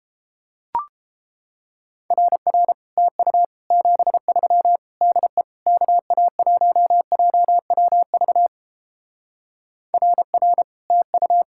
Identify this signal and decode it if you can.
Morse code, heard as a keyed tone, about 29 wpm, 700 Hz. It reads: RR TU 73 DE KA1JWV RR TU